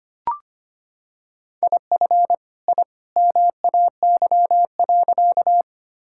Morse code, heard as a keyed tone, about 25 wpm, 700 Hz. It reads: IF I MAY.
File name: practice / 25wpm / Sentences with 2-Word Phrases - 25wpm